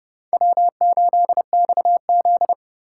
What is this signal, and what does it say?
Morse code, heard as a keyed tone, about 29 wpm, 700 Hz. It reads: W8XZ